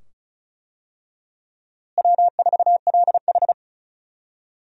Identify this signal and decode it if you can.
Morse code, heard as a keyed tone, about 36 wpm, 700 Hz. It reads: W4LH
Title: USA Call Signs - 4 Characters - Repeated 2x - 35wpm